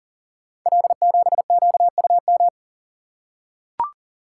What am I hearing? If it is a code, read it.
Morse code, heard as a keyed tone, about 39 wpm, 700 Hz. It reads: L7QUM
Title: Code Groups - 5 Letter:Numbers - 40wpm